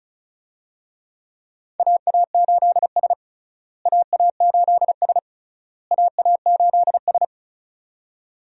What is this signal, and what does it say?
Morse code, heard as a keyed tone, about 34 wpm, 700 Hz. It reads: AA8S AA8S AA8S
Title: USA Call Signs - 4 Characters - Repeated 3x - 35wpm